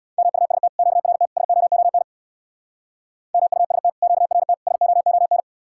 Morse code, heard as a keyed tone, about 60 wpm, 700 Hz. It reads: MUST QRT SOON MUST QRT SOON
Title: Speed-Racing Common QSO Phrases - 50wpm